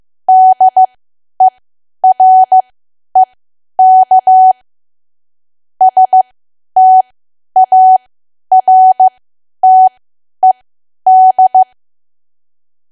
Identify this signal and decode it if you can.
Morse code, heard as a keyed tone, about 15 wpm, spaced as for 10 wpm, 750 Hz. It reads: DEREK STARTED
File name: cwa / W1AW-Level2 / 10WPM